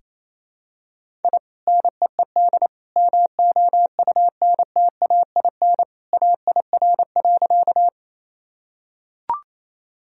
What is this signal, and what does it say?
Morse code, heard as a keyed tone, about 28 wpm, 700 Hz. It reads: I NEED MOUNTAIN AIR.
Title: Sentences from Top 1000 Words - Review Part 1 - 28wpm